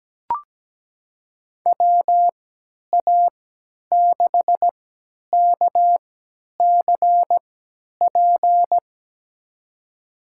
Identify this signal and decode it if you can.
Morse code, heard as a keyed tone, about 17 wpm, spaced as for 10 wpm, 700 Hz. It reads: WA6KCP
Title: USA Call Signs - 6 Characters - 3x Character Spacing - 17wpm